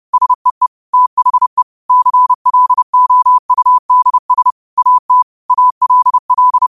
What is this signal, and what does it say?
Morse code, heard as a keyed tone, about 30 wpm, 1000 Hz. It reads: SEE THE CLOUDS AT ALL
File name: qotdfiles / 30wpm